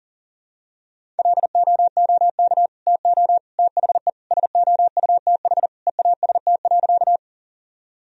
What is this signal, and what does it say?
Morse code, heard as a keyed tone, about 40 wpm, 700 Hz. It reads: LOOK TO THE SOUTH EAST.
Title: Sentences from Top 1000 Words - Review Part 3 - 40wpm